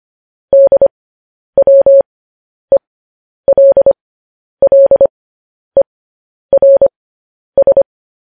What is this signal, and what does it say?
Morse code, heard as a keyed tone, about 25 wpm, spaced as for 10 wpm, 550 Hz. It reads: DWELLERS
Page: News Headlines in Morse Code at 10 WPM